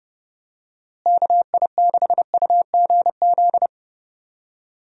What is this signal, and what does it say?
Morse code, heard as a keyed tone, about 30 wpm, 700 Hz. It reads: KI6UGZ
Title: USA Call Signs - 6 Characters - Repeated 2x - 30wpm